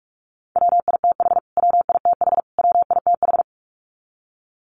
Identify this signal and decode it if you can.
Morse code, heard as a keyed tone, about 44 wpm, 700 Hz. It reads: WITH WITH WITH
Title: Top 300 Words Repeated 45wpm